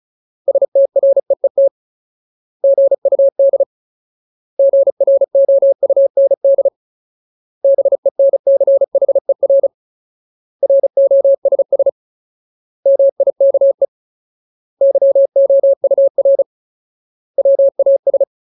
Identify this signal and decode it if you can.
Morse code, heard as a keyed tone, about 34 wpm, 550 Hz. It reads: STREET GUD GROUND BENCHER ROSS MIKE YOUR WAS
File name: Ultimate Word List / 35WPM Ultimate